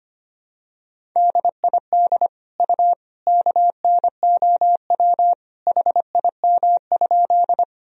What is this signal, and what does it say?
Morse code, heard as a keyed tone, about 25 wpm, 700 Hz. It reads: DID U KNOW HIM?
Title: Sentences from Top 100 Words and Abbreviations 25wpm